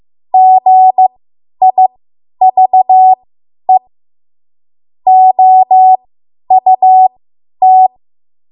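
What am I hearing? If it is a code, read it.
Morse code, heard as a keyed tone, about 15 wpm, spaced as for 10 wpm, 750 Hz. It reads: GIVE OUT
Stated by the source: Morse code practice run at 10 WPM from a code practice archive.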